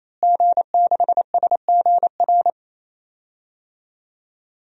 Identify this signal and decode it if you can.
Morse code, heard as a keyed tone, about 28 wpm, 700 Hz. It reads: G6SGR
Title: DXCC-100 Call Signs - 5 Characters - Repeated 3x - 28wpm